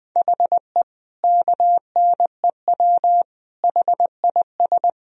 Morse code, heard as a keyed tone, about 20 wpm, 700 Hz. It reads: HE KNEW HIS
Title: Sentences from Top 1000 Words - Review Part 1 - 20wpm